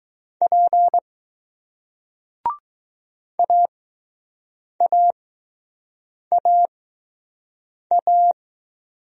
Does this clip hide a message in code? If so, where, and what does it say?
Yes: Morse code from 0.4 to 8.3 s, heard as a keyed tone, about 20 wpm, spaced as for 6 wpm, 700 Hz. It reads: P AAAA